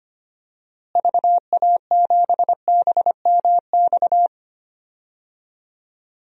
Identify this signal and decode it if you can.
Morse code, heard as a keyed tone, about 25 wpm, 700 Hz. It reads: VA7BMX
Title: DXCC-100 Call Signs - 6 Characters - Repeated 3x - 25wpm